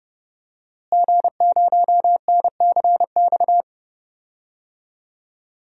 Morse code, heard as a keyed tone, about 30 wpm, 700 Hz. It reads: G0NCX
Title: DXCC-100 Call Signs - 5 Characters - Repeated 3x - 30wpm